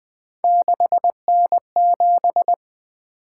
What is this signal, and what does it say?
Morse code, heard as a keyed tone, about 20 wpm, 700 Hz. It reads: 6N7
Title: Three Letter-Number Combinations - 20wpm